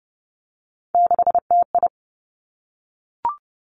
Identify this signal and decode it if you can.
Morse code, heard as a keyed tone, about 32 wpm, 700 Hz. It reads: 6TI
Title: Three Challenging Letter-Number Combinations 30wpm